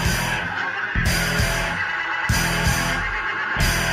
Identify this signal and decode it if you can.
Emergency Alert System (SAME) tones, heard as harsh buzzing bursts. It is heard from 0.3 to 3.6 s.